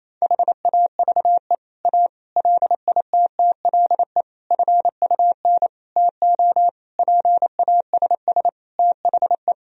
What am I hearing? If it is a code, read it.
Morse code, heard as a keyed tone, about 28 wpm, 700 Hz. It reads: HAVE A LITTLE FUN TO PASS THE
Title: Sentences from Top 1000 Words - Review Part 4 - 28wpm